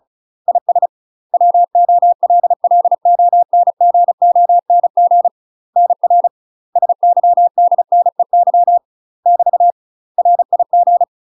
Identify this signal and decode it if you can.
Morse code, heard as a keyed tone, about 34 wpm, 700 Hz. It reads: IS WOLLONGONG NR SYDNEY <BT> RIG
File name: UK MASTER QSOs TRAINING/35WPM